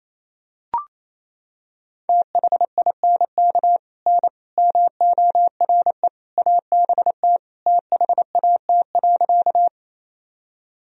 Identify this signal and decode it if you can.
Morse code, heard as a keyed tone, about 28 wpm, 700 Hz. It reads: THINK N MORE ABT THAT.
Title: Sentences from Top 200 Words and Abbreviations 28wpm